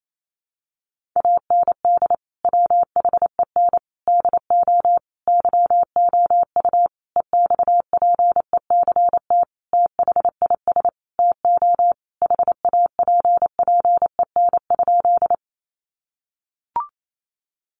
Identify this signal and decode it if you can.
Morse code, heard as a keyed tone, about 28 wpm, 700 Hz. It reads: AND WHEN DO YOU EXPECT THIS TO HAPPEN?
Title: Sentences from Top 1100 Words 28wpm